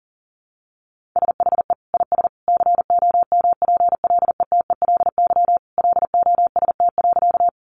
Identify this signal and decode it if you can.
Morse code, heard as a keyed tone, about 40 wpm, 700 Hz. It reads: SHE IS COMPLETELY LOST.
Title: Additional Easy Phrases 40wpm